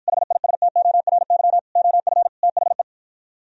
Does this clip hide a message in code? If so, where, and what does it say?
Yes: Morse code from 0.1 to 2.8 s, heard as a keyed tone, about 52 wpm, 700 Hz. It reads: HISTORY OF THE